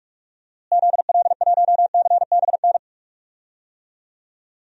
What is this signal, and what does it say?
Morse code, heard as a keyed tone, about 46 wpm, 700 Hz. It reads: ZL1CBN
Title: DXCC-100 Call Signs - 6 Characters - Repeated 3x - 45wpm